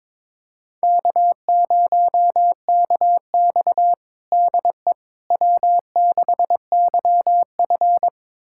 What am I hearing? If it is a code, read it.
Morse code, heard as a keyed tone, about 22 wpm, 700 Hz. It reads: K0KX DE W6YF
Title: USA Callsign DE Callsign - 4 Characters - 22wpm